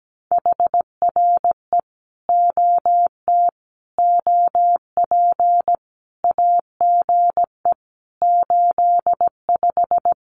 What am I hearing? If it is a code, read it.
Morse code, heard as a keyed tone, about 17 wpm, 700 Hz. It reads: HRE OT OP AGE 85
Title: Common QSO Phrases 17wpm